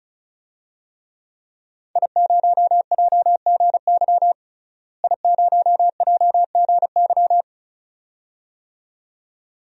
Morse code, heard as a keyed tone, about 34 wpm, 700 Hz. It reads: I0JGY I0JGY
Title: DXCC-100 Call Signs - 5 Characters - Repeated 2x - 35wpm